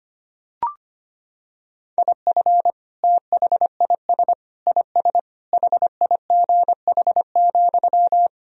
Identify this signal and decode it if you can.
Morse code, heard as a keyed tone, about 25 wpm, 700 Hz. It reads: IF THIS IS HIGH,
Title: Sentences from Top 1000 Words - Review Part 1 - 25wpm